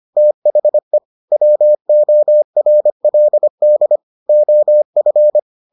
Morse code, heard as a keyed tone, about 25 wpm, 600 Hz. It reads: THE WORLD OF